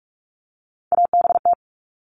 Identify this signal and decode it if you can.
Morse code, heard as a keyed tone, about 46 wpm, 700 Hz. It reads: ABT
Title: Abbreviations Repeated 3x - 45wpm